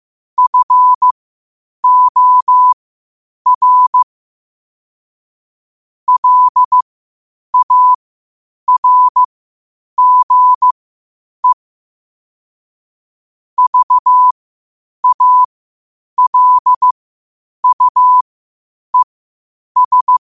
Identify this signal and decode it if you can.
Morse code, heard as a keyed tone, about 15 wpm, spaced as for 8 wpm, 1000 Hz. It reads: FOR LARGE VALUES
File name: qotdfiles / 8wpm